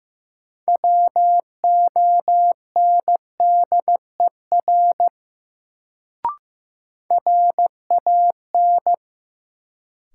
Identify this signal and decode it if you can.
Morse code, heard as a keyed tone, about 16 wpm, 700 Hz. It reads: WONDER RAN
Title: Sentences from Top 500 Words - New Word Drill - 15wpm